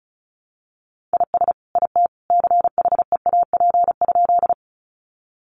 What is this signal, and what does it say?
Morse code, heard as a keyed tone, about 34 wpm, 700 Hz. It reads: IS IT CHEAP?
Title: Sentences from Top 1200 Words 35wpm